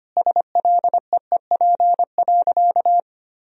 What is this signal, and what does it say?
Morse code, heard as a keyed tone, about 25 wpm, 700 Hz. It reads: SLEEP.
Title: Sentences from Top 1000 Words - Review Part 2 - 25wpm